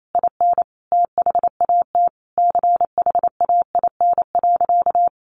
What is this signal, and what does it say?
Morse code, heard as a keyed tone, about 28 wpm, 700 Hz. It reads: IN THAT CHAIN.
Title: Sentences from Top 1500 Words 28wpm